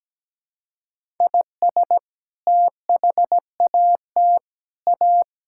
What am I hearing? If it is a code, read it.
Morse code, heard as a keyed tone, about 17 wpm, 700 Hz. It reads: IS THAT A
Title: Sentences from Top 1600 Words 17wpm